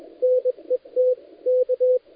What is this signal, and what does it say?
Morse code, heard as a keyed tone, about 21 wpm, 510 Hz, with radio noise. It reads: NET K
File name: dlazaawansowanych / nagraniazpasma / titawaNETcontrol